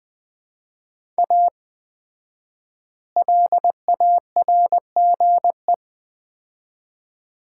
Morse code, heard as a keyed tone, about 20 wpm, 700 Hz. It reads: A LARGE
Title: 4x Word-Spacing - Sets of 4 Words - 20wpm